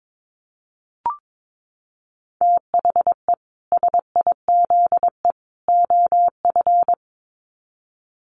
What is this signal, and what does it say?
Morse code, heard as a keyed tone, about 22 wpm, 700 Hz. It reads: THE SIZE OF
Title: Sets of 3 Words 22wpm